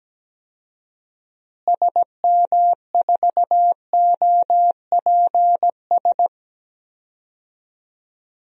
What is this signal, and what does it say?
Morse code, heard as a keyed tone, about 17 wpm, 700 Hz. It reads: SM4OPS
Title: DXCC-100 Call Signs - 6 Characters - Repeated 3x - 17wpm